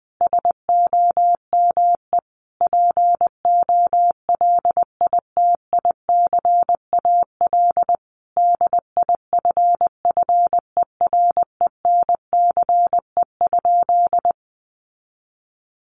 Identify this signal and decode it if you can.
Morse code, heard as a keyed tone, about 20 wpm, 700 Hz. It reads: SOME POLITICAL DIFFERENCE?
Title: Sentences from Top 800 Words 20wpm